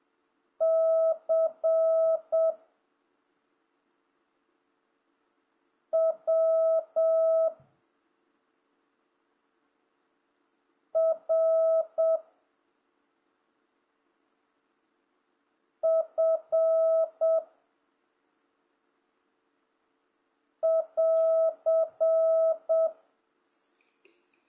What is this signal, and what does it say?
Morse code, heard as a keyed tone, about 7 wpm, 1300 Hz. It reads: C W R F <AR>